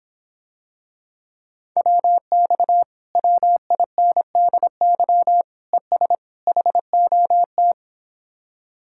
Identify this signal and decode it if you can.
Morse code, heard as a keyed tone, about 26 wpm, 700 Hz. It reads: WX WINDY ES HOT